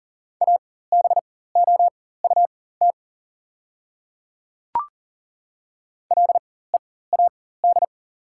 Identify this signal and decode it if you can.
Morse code, heard as a keyed tone, about 40 wpm, spaced as for 19 wpm, 690 Hz. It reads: ABOUT LEAD
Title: 4x Character Spacing - Top 500 Words - 40wpm